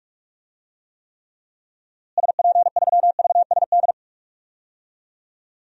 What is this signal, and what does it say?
Morse code, heard as a keyed tone, about 47 wpm, 700 Hz. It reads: SP3VSD
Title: DXCC-100 Call Signs - 6 Characters - Repeated 3x - 45wpm